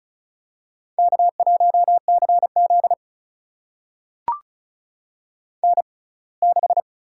Morse code, heard as a keyed tone, about 36 wpm, spaced as for 24 wpm, 700 Hz. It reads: K1CZ N6
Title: USA Call Signs - 4 Characters - 6x Character Spacing - 35wpm